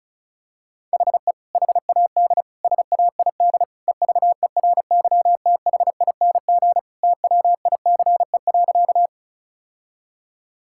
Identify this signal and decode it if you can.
Morse code, heard as a keyed tone, about 36 wpm, 700 Hz. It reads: HE HAD SAID EVERYTHING TWICE.
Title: Sentences from Top 1600 Words 35wpm